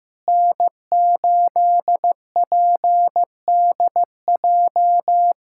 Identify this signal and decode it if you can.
Morse code, heard as a keyed tone, about 15 wpm, 700 Hz. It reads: N8PDJ
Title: USA Callsign DE Callsign - 5 Characters - 15wpm